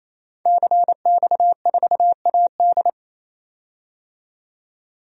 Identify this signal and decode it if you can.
Morse code, heard as a keyed tone, about 28 wpm, 700 Hz. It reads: CX4AD
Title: DXCC-100 Call Signs - 5 Characters - 28wpm